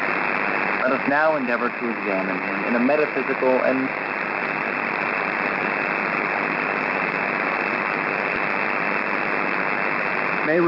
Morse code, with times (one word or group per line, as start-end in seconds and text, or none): none